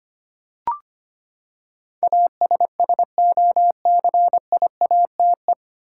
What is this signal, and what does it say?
Morse code, heard as a keyed tone, about 25 wpm, 700 Hz. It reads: ASSOCIATE
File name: Nine Letter Words - 25wpm